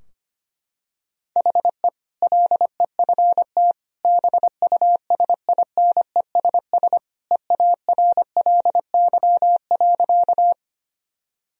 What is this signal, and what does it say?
Morse code, heard as a keyed tone, about 25 wpm, 700 Hz. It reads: HE LEFT BUSINESS EARLY.
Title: Sentences from Top 1000 Words - Review Part 2 - 25wpm